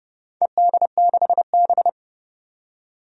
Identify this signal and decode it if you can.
Morse code, heard as a keyed tone, about 30 wpm, 700 Hz. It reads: ED6B
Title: DXCC-100 Call Signs - 4 Characters - 30wpm